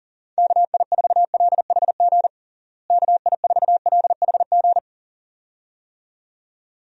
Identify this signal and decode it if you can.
Morse code, heard as a keyed tone, about 40 wpm, 700 Hz. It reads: KI4LHG KI4LHG
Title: USA Call Signs - 6 Characters - Repeated 2x - 40wpm